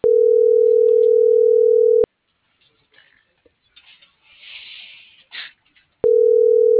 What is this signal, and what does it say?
Ringback tone (North America), 440 + 480 Hz.
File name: CENTRAL NAVIERAS / out leonardo.pe.a